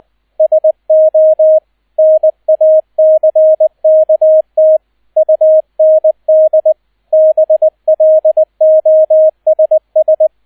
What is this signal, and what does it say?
Morse code, heard as a keyed tone, about 20 wpm, 630 Hz, with radio noise. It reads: SO NACKT UND BLOSS
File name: 002 Zweiter Streich 20wpm